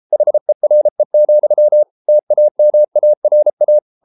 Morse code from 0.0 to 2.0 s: HERE,

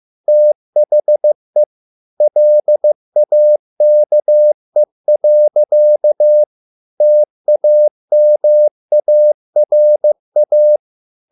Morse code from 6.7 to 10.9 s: TAMARA

